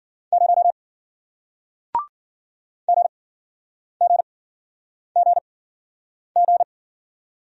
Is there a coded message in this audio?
Yes: Morse code from 0.3 to 6.6 s, heard as a keyed tone, about 50 wpm, spaced as for 9 wpm, 700 Hz. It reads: 0 GGGG